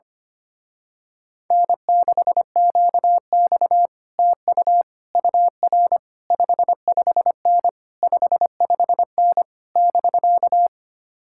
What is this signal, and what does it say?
Morse code, heard as a keyed tone, about 24 wpm, 700 Hz. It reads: N6QX TU UR 55N 55N <BK>
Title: Speed-Racing Common QSO Phrases Encore - 25wpm